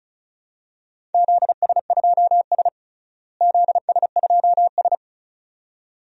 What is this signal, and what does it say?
Morse code, heard as a keyed tone, about 34 wpm, 700 Hz. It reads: ZS2S ZS2S
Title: DXCC-100 Call Signs - 4 Characters - Repeated 2x - 35wpm